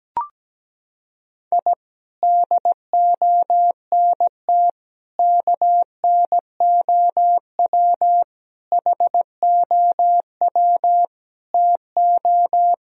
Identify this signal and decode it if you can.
Morse code, heard as a keyed tone, about 17 wpm, 700 Hz. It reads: I DONT KNOW HOW TO
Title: Sentences from Top 1000 Words - Review Part 1 - 17wpm